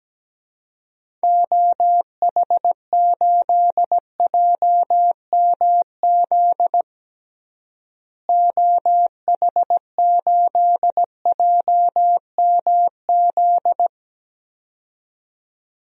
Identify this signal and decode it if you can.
Morse code, heard as a keyed tone, about 17 wpm, 700 Hz. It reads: OH8JMZ OH8JMZ
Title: DXCC-100 Call Signs - 6 Characters - Repeated 2x - 17wpm